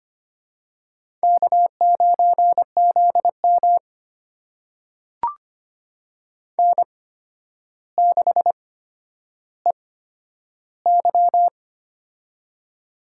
Morse code, heard as a keyed tone, about 25 wpm, spaced as for 11 wpm, 700 Hz. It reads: K9ZM N6EY
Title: USA Call Signs - 4 Characters - 8x Character Spacing - 25wpm